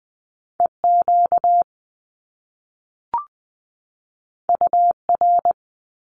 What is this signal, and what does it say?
Morse code, heard as a keyed tone, about 20 wpm, 700 Hz. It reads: E Q U R
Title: Top Two Letter Combinations in Words 20wpm